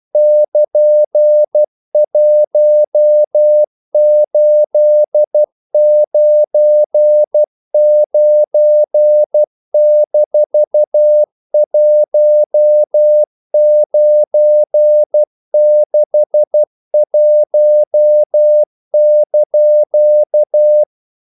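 Morse code, heard as a keyed tone, about 12 wpm, 600 Hz. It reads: <KN>1899-1961)
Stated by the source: Morse practice sending English quotations.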